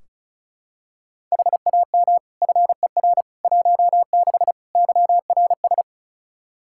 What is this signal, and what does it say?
Morse code, heard as a keyed tone, about 36 wpm, 700 Hz. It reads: HAM FER 16 YRS
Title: Practice QSO - Exchange 3 - More Info - 35wpm